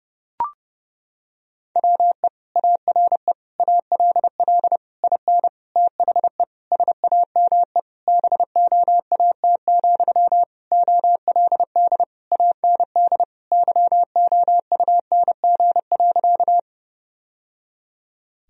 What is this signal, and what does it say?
Morse code, heard as a keyed tone, about 30 wpm, 700 Hz. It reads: WE ARE ALL IN THE SAME BOAT, OLD AND YOUNG.
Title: Sentences from Top 1000 Words - Review Part 3 - 30wpm